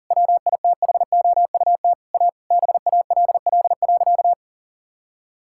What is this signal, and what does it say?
Morse code, heard as a keyed tone, about 37 wpm, 700 Hz. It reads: WITHOUT A BALL.
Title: Sentences from Top 1000 Words - Review Part 6 - 40wpm